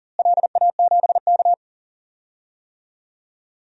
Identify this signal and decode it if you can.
Morse code, heard as a keyed tone, about 39 wpm, 690 Hz. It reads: LA7K
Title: DXCC-100 Call Signs - 4 Characters - Repeated 2x - 40wpm